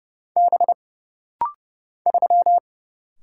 Morse code, heard as a keyed tone, about 31 wpm, 700 Hz. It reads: B 3
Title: Single Letter-Number Alt - Rapid-Fire 30wpm